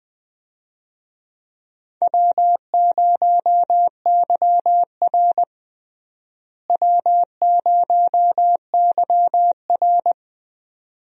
Morse code, heard as a keyed tone, about 20 wpm, 700 Hz. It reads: W0YR W0YR
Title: USA Call Signs - 4 Characters - Repeated 2x - 20wpm